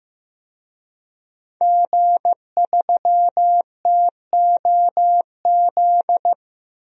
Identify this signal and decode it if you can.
Morse code, heard as a keyed tone, about 15 wpm, 700 Hz. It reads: G3TOZ
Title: DXCC-100 Call Signs - 5 Characters - Repeated 3x - 15wpm